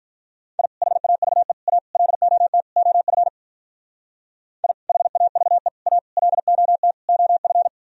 Morse code, heard as a keyed tone, about 44 wpm, 700 Hz. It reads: I HAVE A LOT OF I HAVE A LOT OF